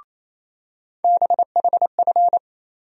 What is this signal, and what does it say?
Morse code, heard as a keyed tone, about 28 wpm, 700 Hz. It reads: BHF
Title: USA Call Sign Postfixes - 3 Letters 28wpm